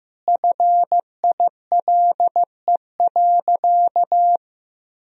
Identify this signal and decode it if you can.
Morse code, heard as a keyed tone, about 15 wpm, 700 Hz. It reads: FILE.
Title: Sentences from Top 1000 Words - Review Part 4 - 15wpm